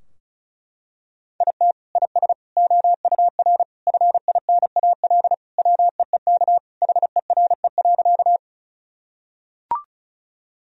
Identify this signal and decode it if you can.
Morse code, heard as a keyed tone, about 34 wpm, 700 Hz. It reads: IT IS OUR FINAL WEEK HERE.
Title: Sentences from Top 1000 Words - Review Part 2 - 35wpm